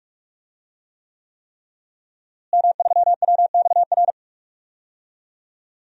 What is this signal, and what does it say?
Morse code, heard as a keyed tone, about 44 wpm, 700 Hz. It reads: M3WXR
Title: DXCC-100 Call Signs - 5 Characters - Repeated 3x - 45wpm